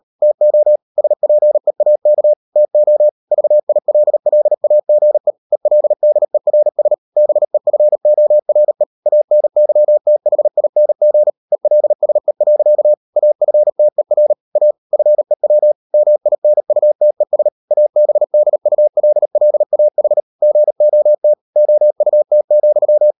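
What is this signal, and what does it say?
Morse code, heard as a keyed tone, about 38 wpm, 600 Hz. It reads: TO SPEAK TO VILLAGE ELDERS BEFORE ANYTHING ELSE. AFTER A FEW MINUTES ABDULLAH GOT OUT,
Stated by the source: Morse code practice at 38 WPM, sending Ken Follett's novel Never in English, from CW med Gnister.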